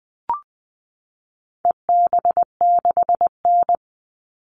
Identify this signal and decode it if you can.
Morse code, heard as a keyed tone, about 20 wpm, 700 Hz. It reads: EB6N
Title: All Cut Numbers - 4-Digit 20wpm